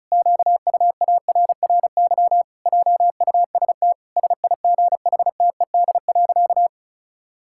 Morse code, heard as a keyed tone, about 36 wpm, 700 Hz. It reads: QUARRY JUST SIGHTED.
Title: Pangram Challenge v2 - 35wpm